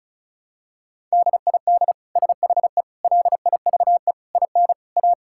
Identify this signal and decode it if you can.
Morse code, heard as a keyed tone, about 36 wpm, 700 Hz. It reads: DID SHE LIVE IN A
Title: Sentences from Top 1000 Words - Review Part 5 - 35wpm